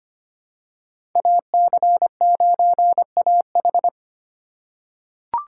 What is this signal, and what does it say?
Morse code, heard as a keyed tone, about 25 wpm, 700 Hz. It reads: AC9AH
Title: USA Call Signs - 5 Characters - 3x Character Spacing - 25wpm